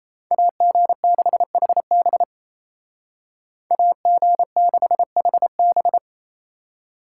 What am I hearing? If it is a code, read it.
Morse code, heard as a keyed tone, about 36 wpm, 700 Hz. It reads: AG6HB AG6HB